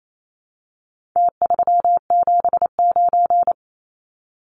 Morse code, heard as a keyed tone, about 28 wpm, 700 Hz. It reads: T379
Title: Numbers - Contest Style - 4-Digit 28wpm